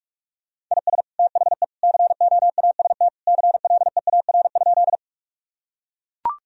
Morse code, heard as a keyed tone, about 42 wpm, 700 Hz. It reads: IS THE COAST CLEAR?
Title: Sentences from Top 1000 Words - Review Part 4 - 45wpm